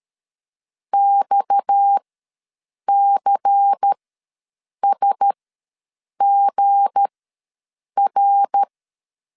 Morse code, heard as a keyed tone, about 13 wpm, spaced as for 7 wpm, 800 Hz. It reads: XCSGR
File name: LEZIONI / 8 YJ45 / 8 YJ ws 12wpm 3342